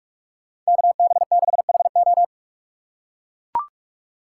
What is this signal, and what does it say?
Morse code, heard as a keyed tone, about 41 wpm, 700 Hz. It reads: KB6HO